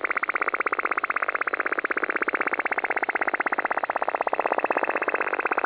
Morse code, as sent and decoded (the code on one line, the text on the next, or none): none